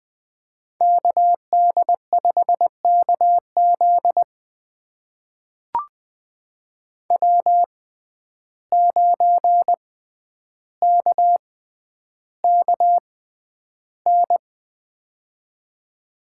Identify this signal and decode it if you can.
Morse code, heard as a keyed tone, about 21 wpm, spaced as for 10 wpm, 700 Hz. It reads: KD5KZ W9KKN
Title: USA Call Signs - 5 Characters - 6x Character Spacing - 20wpm